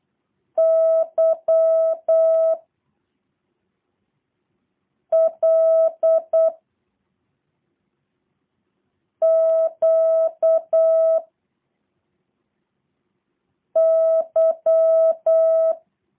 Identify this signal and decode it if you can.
Morse code, heard as a keyed tone, about 8 wpm, 650 Hz. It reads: Y L Q Y
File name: jh8cbh / 2023cw / review1-6